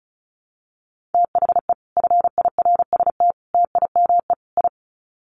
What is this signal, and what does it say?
Morse code, heard as a keyed tone, about 34 wpm, 700 Hz. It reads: THE FIRST TIME I
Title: Sets of 4 Words 35wpm